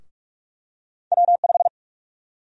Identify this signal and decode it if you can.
Morse code, heard as a keyed tone, about 46 wpm, 700 Hz. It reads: W5
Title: USA Call Sign Prefixes - Repeated 2x - 45wpm